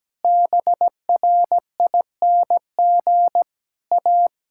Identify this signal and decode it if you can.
Morse code, heard as a keyed tone, about 17 wpm, 700 Hz. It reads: BRING A
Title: Sentences from Top 1000 Words - Review Part 5 - 17wpm